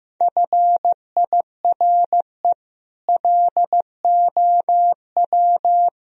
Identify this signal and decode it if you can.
Morse code, heard as a keyed tone, about 15 wpm, 700 Hz. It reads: FIRE LOW
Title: Sentences from Top 1000 Words - Review Part 1 - 15wpm